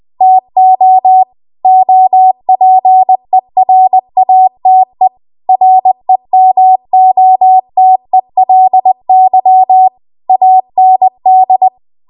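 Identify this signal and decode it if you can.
Morse code, heard as a keyed tone, about 20 wpm, 750 Hz. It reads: TO OPERATE REMOTELY AND
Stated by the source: Morse code practice at 20 WPM.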